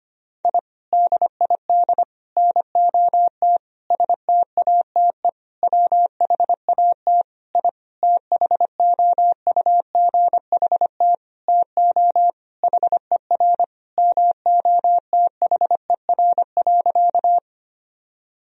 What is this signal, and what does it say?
Morse code, heard as a keyed tone, about 25 wpm, 700 Hz. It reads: I DID NOT STATE WHAT I THOUGHT TO HER MOTHER.
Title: Sentences from Top 1000 Words - Review Part 2 - 25wpm